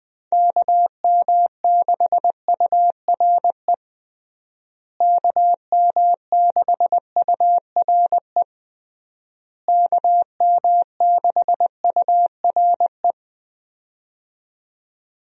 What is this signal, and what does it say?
Morse code, heard as a keyed tone, about 21 wpm, 700 Hz. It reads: KM6URE KM6URE KM6URE
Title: USA Call Signs - 6 Characters - Repeated 3x - 20wpm